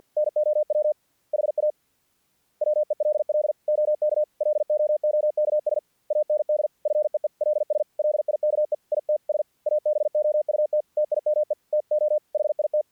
Morse code, heard as a keyed tone, about 48 wpm, 600 Hz. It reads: NOW HA WELL OK LOOKS AND FEELS LIKE ITS ABOUT TIME TO HIT
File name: archived / CRSnet